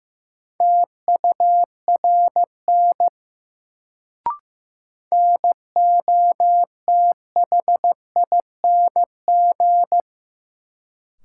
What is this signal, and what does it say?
Morse code, heard as a keyed tone, about 15 wpm, 700 Hz. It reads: TURN NOTHING
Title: Sets of 4 Words - New Word Drill - 15wpm